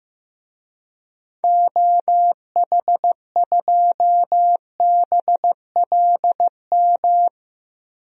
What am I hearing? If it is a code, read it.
Morse code, heard as a keyed tone, about 15 wpm, 700 Hz. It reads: OH2BLM
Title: DXCC-100 Call Signs - 6 Characters - Repeated 2x - 15wpm